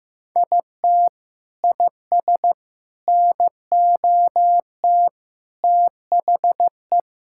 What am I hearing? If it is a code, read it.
Morse code, heard as a keyed tone, about 15 wpm, 700 Hz. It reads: IT IS NOT THE
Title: Sentences from Top 1000 Words - Review Part 5 - 15wpm